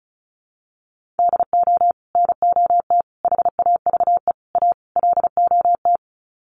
Morse code, heard as a keyed tone, about 34 wpm, 700 Hz. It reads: DO NOT HAVE A LOT
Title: Sets of 5 Words Encore 35wpm